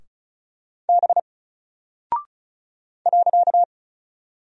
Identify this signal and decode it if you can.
Morse code, heard as a keyed tone, about 35 wpm, 700 Hz. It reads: B .
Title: Single Character (Letter:Number:Punctuation:Prosign) 35wpm